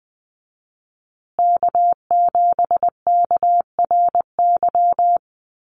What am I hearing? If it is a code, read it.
Morse code, heard as a keyed tone, about 20 wpm, 700 Hz. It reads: K7KRY
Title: USA Call Signs Repeated 2x 20wpm